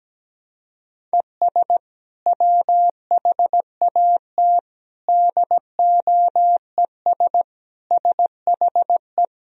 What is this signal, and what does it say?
Morse code, heard as a keyed tone, about 17 wpm, 700 Hz. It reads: ES WHAT DOES SHE
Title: Sentences from Top 400 Words and Abbreviations 17wpm